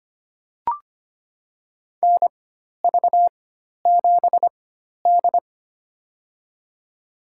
Morse code, heard as a keyed tone, about 25 wpm, spaced as for 12 wpm, 700 Hz. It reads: NV7D
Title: USA Call Signs - 4 Characters - 4x Character Spacing - 25wpm